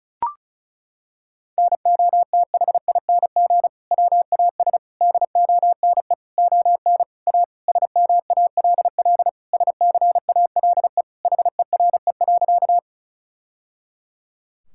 Morse code, heard as a keyed tone, about 36 wpm, 700 Hz. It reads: NOTHING WAS DONE ON A SMALL SCALE HERE.
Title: Sentences from Top 1000 Words - Review Part 5 - 35wpm